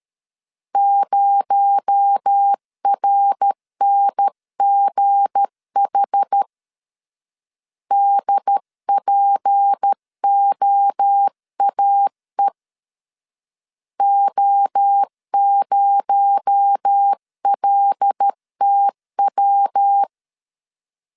Morse code, heard as a keyed tone, about 13 wpm, 800 Hz. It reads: 0RNGH DPOAE O0LTW
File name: LEZIONI / 7 GF9Z / 7 G 12wpm 2055